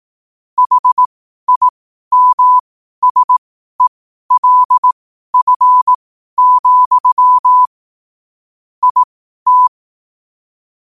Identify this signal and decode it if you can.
Morse code, heard as a keyed tone, about 18 wpm, spaced as for 12 wpm, 1000 Hz. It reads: HIMSELF, IT
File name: qotdfiles / 12wpm